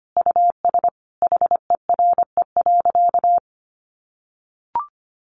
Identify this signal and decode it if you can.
Morse code, heard as a keyed tone, about 25 wpm, 700 Hz. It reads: US HERE.
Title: A Sound Of Thunder 25wpm